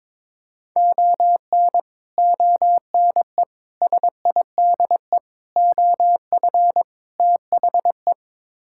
Morse code, heard as a keyed tone, about 21 wpm, 700 Hz. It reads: ON ONE SIDE OF THE
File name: practice / 22wpm / Build A Phrase - 5-Word Phrases - 22wpm